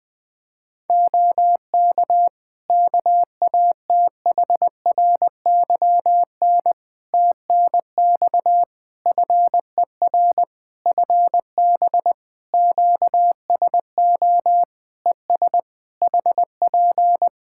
Morse code, heard as a keyed tone, about 20 wpm, 700 Hz. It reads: OK KATHRYN TNX FER FB QSO ES HP
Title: K1USN SST QSO Practice 20wpm